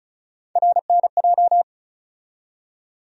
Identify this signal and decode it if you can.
Morse code, heard as a keyed tone, about 34 wpm, 700 Hz. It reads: RNJ
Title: USA Call Sign Postfixes - 3 Letters 35wpm